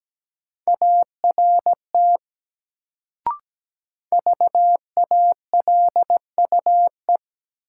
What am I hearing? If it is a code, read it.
Morse code, heard as a keyed tone, about 17 wpm, 700 Hz. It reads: ART VALUE
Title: Sentences from Top 600 Words - New Word Drill - 17wpm